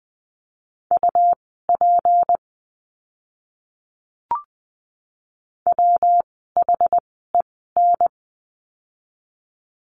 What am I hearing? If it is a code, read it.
Morse code, heard as a keyed tone, about 20 wpm, spaced as for 14 wpm, 700 Hz. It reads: UP WHEN
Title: 192 Lesson 31f - Letter v - Words 20wpm